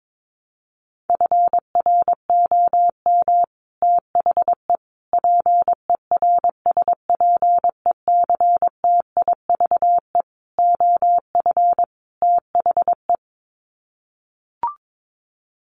Morse code, heard as a keyed tone, about 22 wpm, 700 Hz. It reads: FROM THE PERSPECTIVE OF THE
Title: Sets of 5 Words Encore2 22wpm